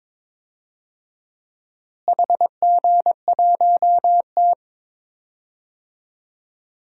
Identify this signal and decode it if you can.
Morse code, heard as a keyed tone, about 21 wpm, 700 Hz. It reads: HG1T
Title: DXCC-100 Call Signs - 4 Characters - Repeated 3x - 22wpm